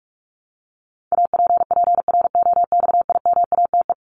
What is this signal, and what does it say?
Morse code, heard as a keyed tone, about 44 wpm, 700 Hz. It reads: APPROXIMATE
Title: LY Suffix - Top 100 Words with Base Words 45wpm